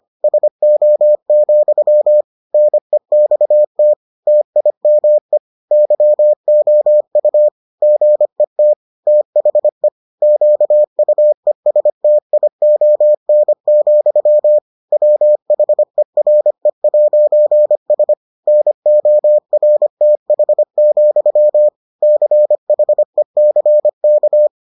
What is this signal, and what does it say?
Morse code, heard as a keyed tone, about 25 wpm, 600 Hz. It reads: SO, NEXT TIME YOU GET THE QUESTION, WHERE'S NORTH, CHECK